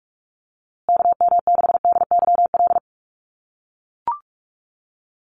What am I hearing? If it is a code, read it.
Morse code, heard as a keyed tone, about 44 wpm, 700 Hz. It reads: KM6DYL
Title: USA Call Signs Repeated 3x 45wpm